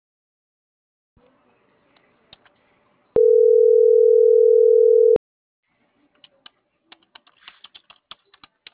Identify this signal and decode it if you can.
Ringback tone (North America), 440 + 480 Hz.